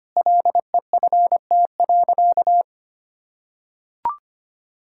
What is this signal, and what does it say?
Morse code, heard as a keyed tone, about 25 wpm, 700 Hz. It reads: LEFT.
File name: practice / 25wpm / Sentences with 3-Word Phrases - 25wpm